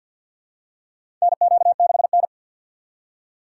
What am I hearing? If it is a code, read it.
Morse code, heard as a keyed tone, about 50 wpm, 700 Hz. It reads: NQ6N
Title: USA Call Signs - 4 Characters - 6x Character Spacing - 50wpm